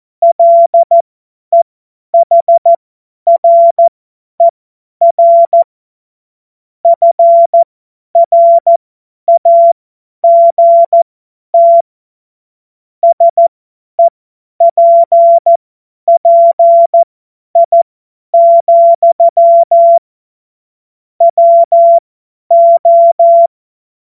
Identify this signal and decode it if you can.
Morse code, heard as a keyed tone, about 14 wpm, spaced as for 10 wpm, 680 Hz. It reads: LEHRER FRAGT SEPPI, WO
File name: Morsekurs / Witze / Witz 07. Februar 2023 50 F70